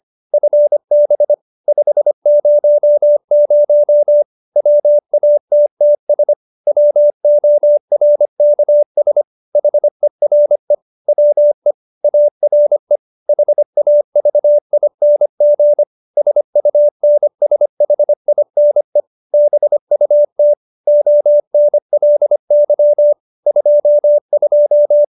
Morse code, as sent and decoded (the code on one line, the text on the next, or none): ..-. -... / ..... ----- ----- / .-- .- - - ... / .-- --- .-. -.- ... / .... .-. / .-- . / .- .-. / .... .- ...- .. -. --. / ... ..- -. ... .... .. -. / -... ..- - / --- -. .-.. -.-- / ..--- ..---
FB 500 WATTS WORKS HERE WE ARE HAVING SUNSHINE BUT ONLY 22